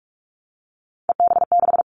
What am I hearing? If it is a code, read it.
Morse code, heard as a keyed tone, about 46 wpm, 700 Hz. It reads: EB6
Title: Callsign Prefixes for Top 50 DXCC - 45wpm